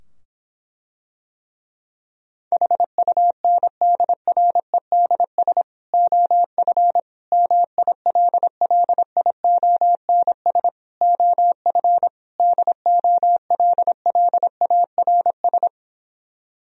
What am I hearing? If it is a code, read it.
Morse code, heard as a keyed tone, about 26 wpm, 700 Hz. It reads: HUNDREDS OF MILLIONS OF DOLLARS